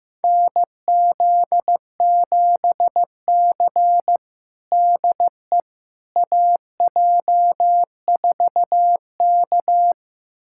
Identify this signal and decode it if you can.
Morse code, heard as a keyed tone, about 15 wpm, 700 Hz. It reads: NZ7C DE AJ4K
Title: USA Callsign DE Callsign - 4 Characters - 15wpm